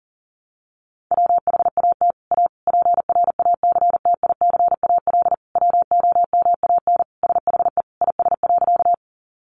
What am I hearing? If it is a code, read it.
Morse code, heard as a keyed tone, about 40 wpm, 700 Hz. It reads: WHAT A PRACTICAL WOMAN SHE IS.
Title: Sentences from Top 1400 Words 40wpm